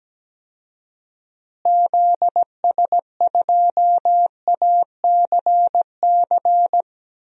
Morse code, heard as a keyed tone, about 17 wpm, 700 Hz. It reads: ZS2ACC
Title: DXCC-100 Call Signs - 6 Characters - Repeated 3x - 17wpm